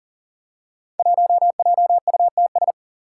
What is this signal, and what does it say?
Morse code, heard as a keyed tone, about 37 wpm, 700 Hz. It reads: 1JUTS
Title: Five Challenging Letter-Number Combinations 40wpm